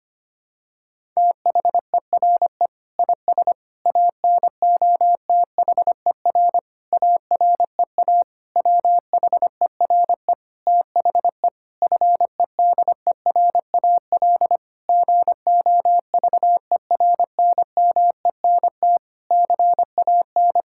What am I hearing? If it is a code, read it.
Morse code, heard as a keyed tone, about 25 wpm, 700 Hz. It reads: THERE IS ANOTHER AREA WHERE THE FEDERAL GOVERNMENT CAN
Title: Sentences from Top 1000 Words - Review Part 5 - 25wpm